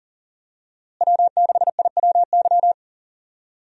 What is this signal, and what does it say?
Morse code, heard as a keyed tone, about 37 wpm, 700 Hz. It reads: W6IWY
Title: USA Call Signs - 5 Characters - Repeated 2x - 40wpm